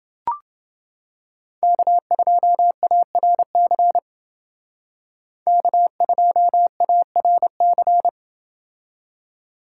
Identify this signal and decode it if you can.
Morse code, heard as a keyed tone, about 28 wpm, 700 Hz. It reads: K2ARC K2ARC